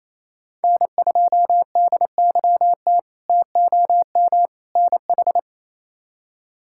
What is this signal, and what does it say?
Morse code, heard as a keyed tone, about 28 wpm, 700 Hz. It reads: N2DYT TOM NH
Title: K1USN SST QSO Practice - 28wpm